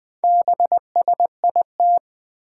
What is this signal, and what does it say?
Morse code, heard as a keyed tone, about 21 wpm, 700 Hz. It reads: BSIT
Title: Four Challenging Letter-Number Combinations 20wpm